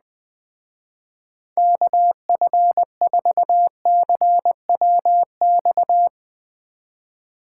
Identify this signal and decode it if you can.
Morse code, heard as a keyed tone, about 20 wpm, 700 Hz. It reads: KF4CWX
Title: Speed Racing - USA 2x3 Call Signs 17wpm